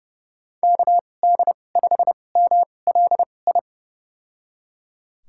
Morse code, heard as a keyed tone, about 30 wpm, spaced as for 22 wpm, 700 Hz. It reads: KD5MLI
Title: USA Call Signs - 6 Characters - 2x Character Spacing - 30wpm